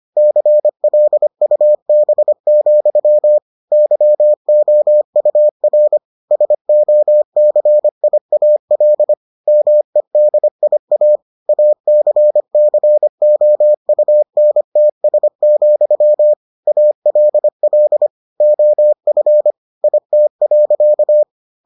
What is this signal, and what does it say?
Morse code, heard as a keyed tone, about 25 wpm, 600 Hz. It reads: CLUB, YOUR SOCIAL MEDIA ACCOUNTS, ALL OF IT.